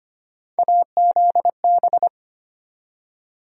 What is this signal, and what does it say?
Morse code, heard as a keyed tone, about 25 wpm, 700 Hz. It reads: AZB
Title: USA Call Sign Postfixes - 3 Letters 25wpm